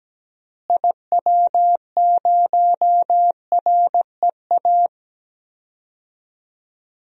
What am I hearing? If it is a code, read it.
Morse code, heard as a keyed tone, about 17 wpm, 700 Hz. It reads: IW0REA
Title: DXCC-100 Call Signs - 6 Characters - Repeated 2x - 17wpm